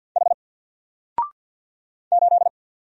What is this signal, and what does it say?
Morse code, heard as a keyed tone, about 48 wpm, 700 Hz. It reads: H 8
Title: Single Letter-Number Alt 50wpm